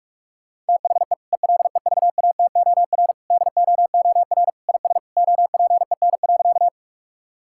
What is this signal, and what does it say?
Morse code, heard as a keyed tone, about 46 wpm, 700 Hz. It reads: THE ELEVATOR DOOR IS OPEN.
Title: Sentences from Top 1600 Words 45wpm